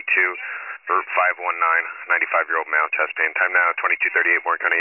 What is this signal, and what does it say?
Emergency Alert System (SAME) tones, heard as harsh buzzing bursts, with speech. It is heard from 0.2 to 3.7 s.